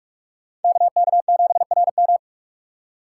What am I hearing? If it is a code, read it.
Morse code, heard as a keyed tone, about 44 wpm, 690 Hz. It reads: KK7RM
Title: USA Call Signs - 5 Characters - 6x Character Spacing - 45wpm